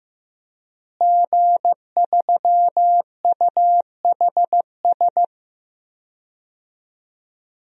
Morse code, heard as a keyed tone, about 15 wpm, 700 Hz. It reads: G3UHS
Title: DXCC-100 Call Signs - 5 Characters - Repeated 2x - 15wpm